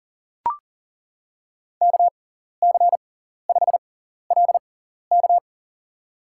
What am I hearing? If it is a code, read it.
Morse code, heard as a keyed tone, about 40 wpm, spaced as for 14 wpm, 700 Hz. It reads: KC5LK
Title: USA Call Signs - 5 Characters - 6x Character Spacing - 40wpm